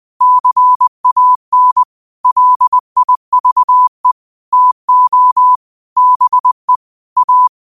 Morse code, heard as a keyed tone, about 20 wpm, 1000 Hz. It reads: CAN LIVE TO BE A
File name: qotdfiles / 20wpm